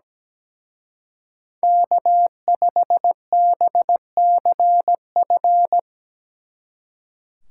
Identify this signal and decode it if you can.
Morse code, heard as a keyed tone, about 18 wpm, 700 Hz. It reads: K5BCF
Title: Speed Racing - USA Call Signs 17wpm